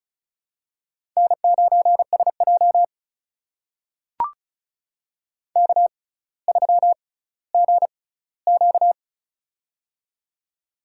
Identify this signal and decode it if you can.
Morse code, heard as a keyed tone, about 36 wpm, spaced as for 18 wpm, 700 Hz. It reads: N9SJ K3GQ
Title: USA Call Signs - 4 Characters - 6x Character Spacing - 35wpm